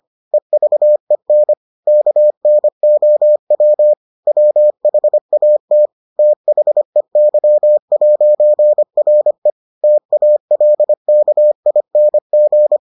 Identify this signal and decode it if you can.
Morse code, heard as a keyed tone, about 25 wpm, 600 Hz. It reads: EVEN KNOW WHAT THEY'RE TALKING